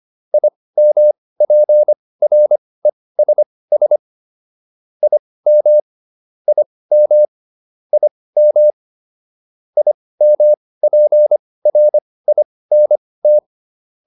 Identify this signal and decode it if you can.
Morse code, heard as a keyed tone, about 25 wpm, spaced as for 18 wpm, 600 Hz. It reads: IMPRESS IM IM IM IMPRINT